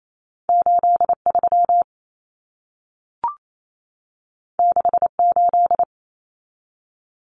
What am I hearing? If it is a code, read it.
Morse code, heard as a keyed tone, about 28 wpm, 700 Hz. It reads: 83 68
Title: Numbers - 2-Digit 28wpm